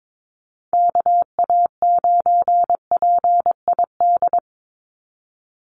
Morse code, heard as a keyed tone, about 22 wpm, 700 Hz. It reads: KA9PID
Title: USA Call Signs 22wpm